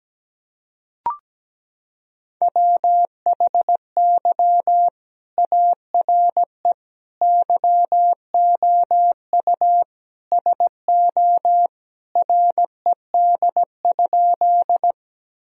Morse code, heard as a keyed tone, about 17 wpm, 700 Hz. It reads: WHY ARE YOU SO RED?
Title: Sentences from Top 1000 Words - Review Part 2 - 17wpm